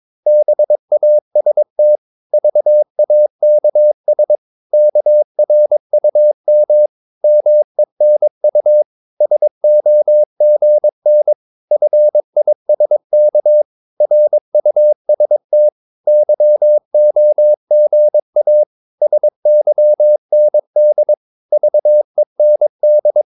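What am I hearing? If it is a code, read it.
Morse code, heard as a keyed tone, about 22 wpm, 600 Hz. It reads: BAST VAKS KRUM MENU SOGN FISK RUST YOGA SYND VEND
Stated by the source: Morse practice of short Danish words at 22 wpm, from CW med Gnister.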